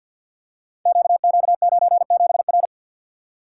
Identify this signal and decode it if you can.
Morse code, heard as a keyed tone, about 47 wpm, 700 Hz. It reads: QQ97R
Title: Code Groups - 5 Letter:Numbers - 50wpm